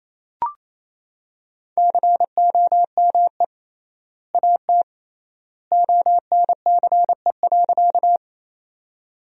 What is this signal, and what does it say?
Morse code, heard as a keyed tone, about 28 wpm, 700 Hz. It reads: COME AT ONCE.
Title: 3x Word-Spacing - Sentences from Top 300 Words - 28wpm